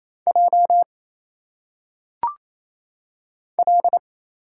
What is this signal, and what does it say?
Morse code, heard as a keyed tone, about 27 wpm, 700 Hz. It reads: J L
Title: Single Letters - Phonetic Pronunciation 28wpm